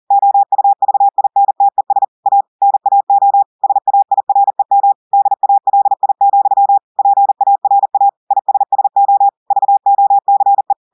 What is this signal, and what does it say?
Morse code, heard as a keyed tone, about 40 wpm, 800 Hz. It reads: OUVINTES A NAO SAIREM DALI, PARA ISSO VOCE
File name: audios / portugais / portugais texte en clair 2 vitesse 40 mots minutes